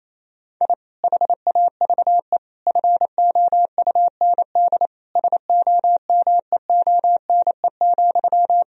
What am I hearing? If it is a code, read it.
Morse code, heard as a keyed tone, about 28 wpm, 700 Hz. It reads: I HAVE FOUND SOMEONE,
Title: Sentences from Top 1000 Words - Review Part 1 - 28wpm